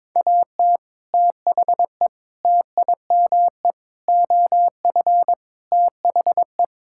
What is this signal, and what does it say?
Morse code, heard as a keyed tone, about 22 wpm, 700 Hz. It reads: AT THE TIME OF THE
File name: practice / 22wpm / Sentences with 4-Word Phrases - 22wpm